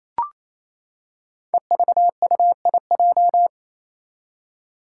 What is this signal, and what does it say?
Morse code, heard as a keyed tone, about 28 wpm, 700 Hz. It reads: EVUIJ
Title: Five Challenging Letter-Number Combinations 28wpm